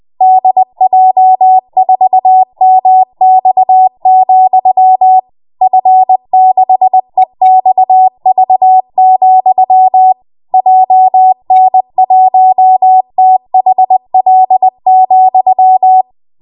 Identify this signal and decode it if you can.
Morse code, heard as a keyed tone, about 20 wpm, 750 Hz. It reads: DJ4MX, F6EXV, JN1THL,
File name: Morse / 20WPM